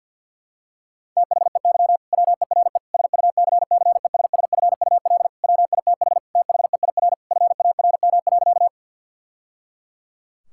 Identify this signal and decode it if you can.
Morse code, heard as a keyed tone, about 50 wpm, 700 Hz. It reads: THEY WERE SUCCESSFUL WITH THEIR FARM.
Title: Sentences from Top 1000 Words - Review Part 6 - 50wpm